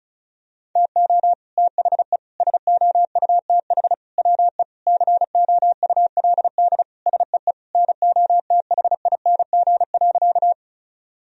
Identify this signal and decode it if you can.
Morse code, heard as a keyed tone, about 34 wpm, 700 Hz. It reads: TO THE SOUTH WE COULD SEE NOTHING.
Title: Sentences from Top 1000 Words - Review Part 2 - 35wpm